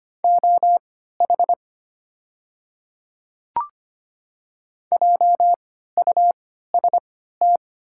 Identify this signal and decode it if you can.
Morse code, heard as a keyed tone, about 25 wpm, spaced as for 14 wpm, 690 Hz. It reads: OH JUST
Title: 3x Character Spacing - Top 500 Words - 25wpm